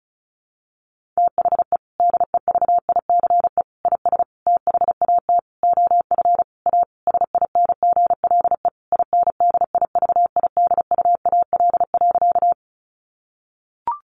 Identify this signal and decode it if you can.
Morse code, heard as a keyed tone, about 34 wpm, 700 Hz. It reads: THE DEVICE IS THAT OF A SINGLE INDIVIDUAL.
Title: Sentences from Top 1500 Words 35wpm